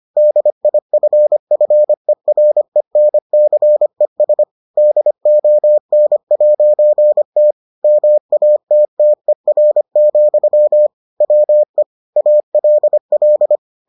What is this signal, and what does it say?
Morse code, heard as a keyed tone, about 25 wpm, 600 Hz. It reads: DIFFERENCES DON'T MATTER, WE ALL